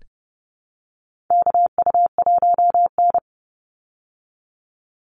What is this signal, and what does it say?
Morse code, heard as a keyed tone, about 30 wpm, 700 Hz. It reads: KU1N
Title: Call Signs Repeated 2x - 30wpm